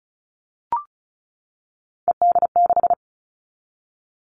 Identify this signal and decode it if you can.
Morse code, heard as a keyed tone, about 34 wpm, 700 Hz. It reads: ED6
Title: Callsign Prefixes for Top 100 DXCC - 35wpm